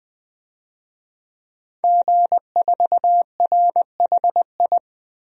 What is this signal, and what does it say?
Morse code, heard as a keyed tone, about 20 wpm, 700 Hz. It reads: G4RHI
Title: DXCC-100 Call Signs - 5 Characters - 20wpm